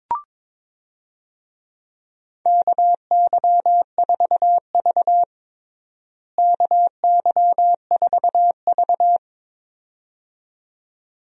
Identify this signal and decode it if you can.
Morse code, heard as a keyed tone, about 22 wpm, 700 Hz. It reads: KY4V KY4V
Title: USA Call Signs - 4 Characters - Repeated 2x - 22wpm